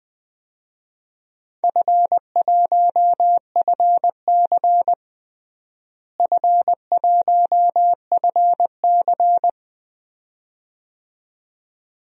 Morse code, heard as a keyed tone, about 20 wpm, 700 Hz. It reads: F1FC F1FC
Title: DXCC-100 Call Signs - 4 Characters - Repeated 2x - 20wpm